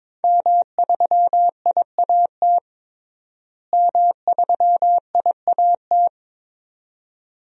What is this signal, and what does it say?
Morse code, heard as a keyed tone, about 22 wpm, 700 Hz. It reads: M3IAT M3IAT
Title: DXCC-100 Call Signs - 5 Characters - Repeated 2x - 22wpm